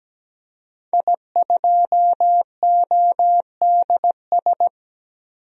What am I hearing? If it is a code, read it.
Morse code, heard as a keyed tone, about 17 wpm, 700 Hz. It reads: I2ODS
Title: DXCC-100 Call Signs - 5 Characters - 17wpm